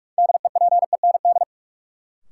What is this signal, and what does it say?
Morse code, heard as a keyed tone, about 46 wpm, 700 Hz. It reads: DEPEND
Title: Sentences from Top 1300 Words - New Word Drill - 45wpm